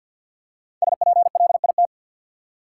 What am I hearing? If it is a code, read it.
Morse code, heard as a keyed tone, about 49 wpm, 700 Hz. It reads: SPLIT
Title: Five Letter Words - 50wpm